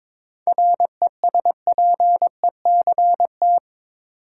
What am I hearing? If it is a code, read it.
Morse code, heard as a keyed tone, about 22 wpm, 700 Hz. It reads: RESPECT
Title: Seven Letter Words - 22wpm